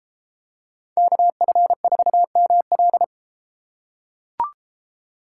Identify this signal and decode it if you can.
Morse code, heard as a keyed tone, about 33 wpm, 700 Hz. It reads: KF4ML